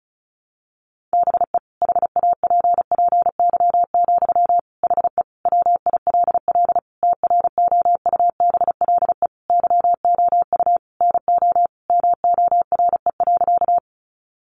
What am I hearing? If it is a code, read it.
Morse code, heard as a keyed tone, about 34 wpm, 700 Hz. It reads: BE HAPPY, HE WILL TROUBLE YOU NO MORE.
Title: Sentences from Top 1200 Words 35wpm